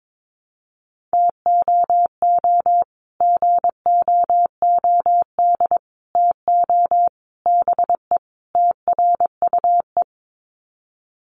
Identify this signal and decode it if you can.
Morse code, heard as a keyed tone, about 22 wpm, 700 Hz. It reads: TOO GOOD TO BE TRUE
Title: Sets of 5 Words Encore 22wpm